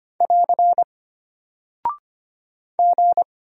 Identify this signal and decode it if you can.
Morse code, heard as a keyed tone, about 25 wpm, 700 Hz. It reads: <AR> G
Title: Single Character (Letter:Number:Punctuation:Prosign) Alt 25wpm